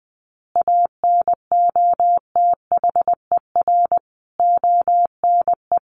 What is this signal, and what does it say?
Morse code, heard as a keyed tone, about 20 wpm, 700 Hz. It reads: ANOTHER ONE
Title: Additional Easy Phrases 20wpm